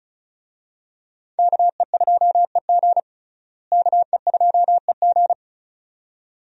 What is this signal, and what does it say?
Morse code, heard as a keyed tone, about 36 wpm, 700 Hz. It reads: KE2EG KE2EG
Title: USA Call Signs - 5 Characters - Repeated 2x - 35wpm